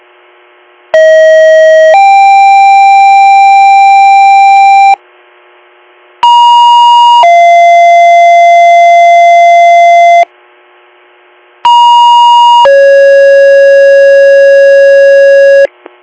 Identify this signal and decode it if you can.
Two-tone page, heard as a one-second tone, then a three-second tone, about 640 then 790 Hz, about 950 then 690 Hz, about 950 then 550 Hz, with noise. It is heard from 0.9 to 15.7 s.